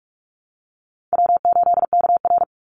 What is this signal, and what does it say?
Morse code, heard as a keyed tone, about 44 wpm, 700 Hz. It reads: W8KR
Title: Call Signs - 45wpm